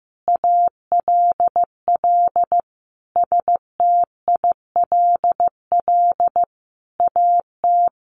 Morse code, heard as a keyed tone, about 15 wpm, 700 Hz. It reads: ALL STILL AT
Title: Sentences from Top 1500 Words 15wpm